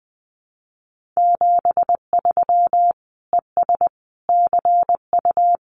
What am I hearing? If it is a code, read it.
Morse code, heard as a keyed tone, about 20 wpm, 700 Hz. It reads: 73 ES CU
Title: Common QSO Phrases Encore 20wpm